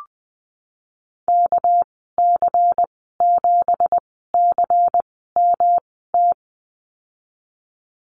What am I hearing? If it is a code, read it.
Morse code, heard as a keyed tone, about 20 wpm, spaced as for 15 wpm, 700 Hz. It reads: KC7CMT
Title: 200 Lesson 32g - Letter k - Call Signs 20wpm